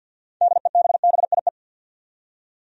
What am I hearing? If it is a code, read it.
Morse code, heard as a keyed tone, about 52 wpm, 700 Hz. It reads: DEBBIE
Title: USA Common Names - 50wpm